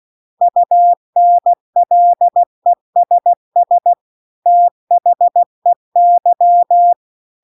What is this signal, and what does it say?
Morse code, heard as a keyed tone, about 16 wpm, 700 Hz. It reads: UNLESS THEY
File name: CW-The-Big-Twoheaded-River-2-16WPM